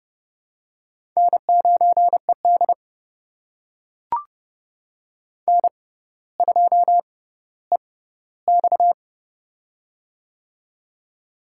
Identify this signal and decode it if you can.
Morse code, heard as a keyed tone, about 30 wpm, spaced as for 15 wpm, 700 Hz. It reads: N9ED N2EX
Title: USA Call Signs - 4 Characters - 6x Character Spacing - 30wpm